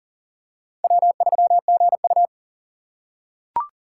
Morse code, heard as a keyed tone, about 40 wpm, 700 Hz. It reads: W3GU
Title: USA Call Signs - 2x Character Spacing - 40wpm